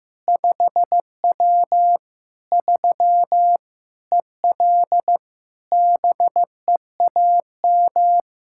Morse code, heard as a keyed tone, about 15 wpm, 700 Hz. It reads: 5W 3 EL BEAM
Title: K1USN SST QSO Practice 15wpm